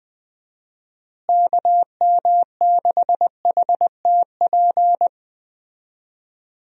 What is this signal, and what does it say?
Morse code, heard as a keyed tone, about 20 wpm, 700 Hz. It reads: KM6HTP
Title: USA Call Signs - 6 Characters - Repeated 2x - 20wpm